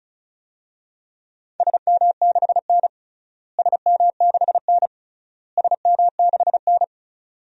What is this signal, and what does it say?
Morse code, heard as a keyed tone, about 36 wpm, 700 Hz. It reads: SM6N SM6N SM6N
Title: DXCC-100 Call Signs - 4 Characters - Repeated 3x - 35wpm